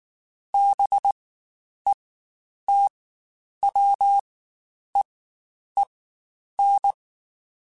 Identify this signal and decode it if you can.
Morse code, heard as a keyed tone, about 19 wpm, spaced as for 9 wpm, 800 Hz. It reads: BETWEEN
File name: Spaced Characters / Spaced Characters-mp3s / Weather5 22 sp @18